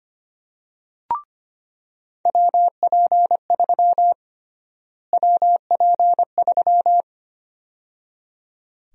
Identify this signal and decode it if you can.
Morse code, heard as a keyed tone, about 25 wpm, 700 Hz. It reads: WP3 WP3
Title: USA Call Sign Prefixes - Repeated 2x - 25wpm